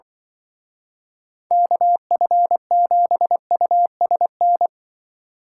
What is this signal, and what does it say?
Morse code, heard as a keyed tone, about 23 wpm, 700 Hz. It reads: KF7USN
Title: Speed Racing - USA 2x3 Call Signs 20wpm